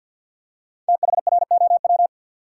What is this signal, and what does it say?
Morse code, heard as a keyed tone, about 53 wpm, 700 Hz. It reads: THROW
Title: Five Letter Words - 50wpm